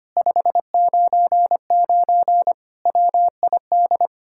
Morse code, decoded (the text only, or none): 599 WID